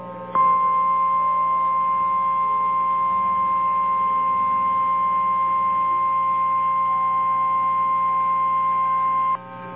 Emergency Alert System (SAME) tones (the 1050 Hz attention tone). It is heard from 0.3 to 9.4 s.
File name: eas / KNCYKBIE / KNCYKBIE